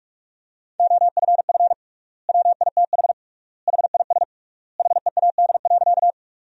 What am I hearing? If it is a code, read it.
Morse code, heard as a keyed tone, about 41 wpm, 700 Hz. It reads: OFF WITH HIS HEAD.
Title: 3x Word-Spacing - Sentences from Top 300 Words - 45wpm